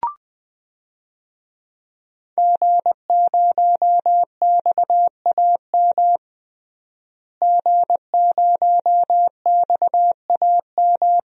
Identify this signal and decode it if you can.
Morse code, heard as a keyed tone, about 20 wpm, 700 Hz. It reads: G0XAM G0XAM
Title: DXCC-100 Call Signs - 5 Characters - Repeated 3x - 20wpm